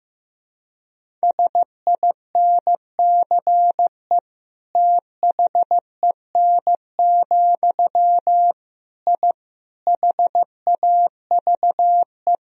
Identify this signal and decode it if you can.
Morse code, heard as a keyed tone, about 15 wpm, 700 Hz. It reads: SINCE THEN, I HAVE
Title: Sentences from Top 1000 Words - Review Part 1 - 15wpm